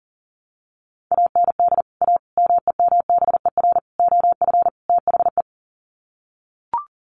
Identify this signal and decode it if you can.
Morse code, heard as a keyed tone, about 40 wpm, 700 Hz. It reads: AND A MEMBER OF THE
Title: Sets of 5 Words 40wpm